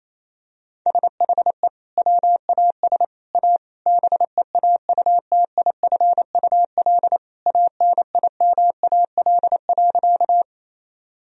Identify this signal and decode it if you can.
Morse code, heard as a keyed tone, about 28 wpm, 700 Hz. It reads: SHE WAS A BEAUTIFUL ANIMAL.
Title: Sentences from Top 1000 Words - Review Part 1 - 28wpm